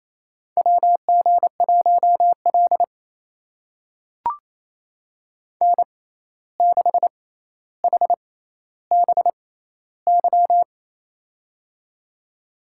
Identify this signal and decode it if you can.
Morse code, heard as a keyed tone, about 28 wpm, spaced as for 13 wpm, 700 Hz. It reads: WG1L N6HBY
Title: USA Call Signs - 6x Character Spacing - 28wpm